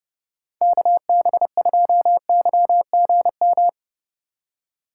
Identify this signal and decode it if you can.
Morse code, heard as a keyed tone, about 30 wpm, 700 Hz. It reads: KB2YGM
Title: USA Call Signs - 6 Characters - 6x Character Spacing - 30wpm